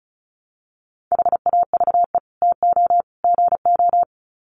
Morse code, heard as a keyed tone, about 34 wpm, 700 Hz. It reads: HAVE TO GO
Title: Sets of 3 Words 35wpm